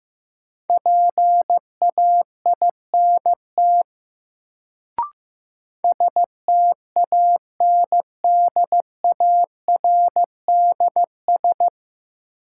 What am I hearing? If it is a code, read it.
Morse code, heard as a keyed tone, about 15 wpm, 700 Hz. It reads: PAINT STANDARDS
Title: Sentences from Top 1000 Words - Review Part 6 - New Word Drill - 15wpm